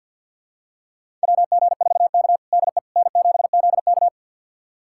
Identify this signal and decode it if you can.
Morse code, heard as a keyed tone, about 47 wpm, 700 Hz. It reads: WG4K DE N7ZK
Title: USA Callsign DE Callsign - 4 Characters - 50wpm